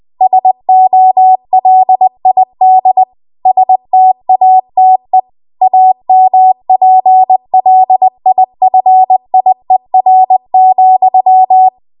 Morse code, heard as a keyed tone, about 20 wpm, 750 Hz. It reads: SOLID STATE AMPLIFIER,
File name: Morse / 20WPM